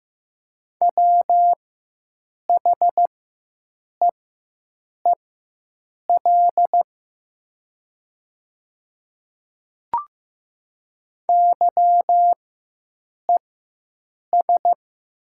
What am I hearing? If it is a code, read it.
Morse code, heard as a keyed tone, about 15 wpm, spaced as for 7 wpm, 700 Hz. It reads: WHEEL YES
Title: 4x Character Spacing - Top 500 Words - 15wpm